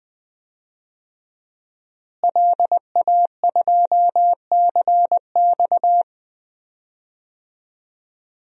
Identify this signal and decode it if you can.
Morse code, heard as a keyed tone, about 20 wpm, 700 Hz. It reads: LA2CX
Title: DXCC-100 Call Signs - 5 Characters - Repeated 2x - 20wpm